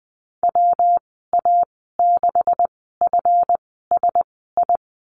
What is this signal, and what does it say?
Morse code, heard as a keyed tone, about 20 wpm, spaced as for 14 wpm, 700 Hz. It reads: WA6FSI
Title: 193 Lesson 31g - Letter v - Call Signs 20wpm